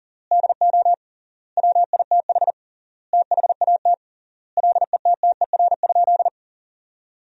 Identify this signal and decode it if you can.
Morse code, heard as a keyed tone, about 39 wpm, 700 Hz. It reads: DO WITH THAT LETTER?
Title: 3x Word-Spacing - Sentences from Top 300 Words - 40wpm